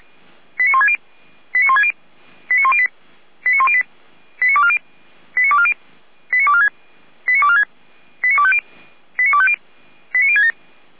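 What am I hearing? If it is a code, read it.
Five-tone selective call (ZVEI), heard as quick runs of five tones, tones of about 70 ms. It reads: 89160 89160 89108 89108 89240 89240 89246 89246 89250 89250 89067
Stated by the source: ZVEI five-tone selective call received on an SDR.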